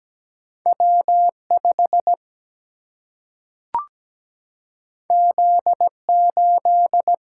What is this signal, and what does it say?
Morse code, heard as a keyed tone, about 17 wpm, 700 Hz. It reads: W 5 Z 8
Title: Single-Letter Single-Number 17wpm